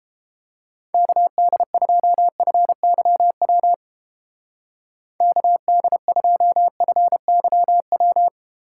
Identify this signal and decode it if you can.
Morse code, heard as a keyed tone, about 30 wpm, 700 Hz. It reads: KD2FYW KD2FYW